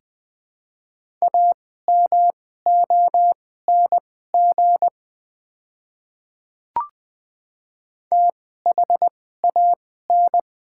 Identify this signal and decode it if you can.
Morse code, heard as a keyed tone, about 20 wpm, spaced as for 14 wpm, 700 Hz. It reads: AMONG THAN
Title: 2x Character Spacing - Top 500 Words - 20wpm